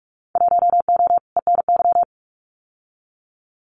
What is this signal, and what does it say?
Morse code, heard as a keyed tone, about 46 wpm, 700 Hz. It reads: 1O ENY
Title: Winter Field Day Exchanges 45wpm